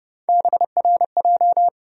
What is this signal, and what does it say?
Morse code, heard as a keyed tone, about 30 wpm, 700 Hz. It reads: BRJ
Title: USA Call Sign Postfixes - 3 Letters 30wpm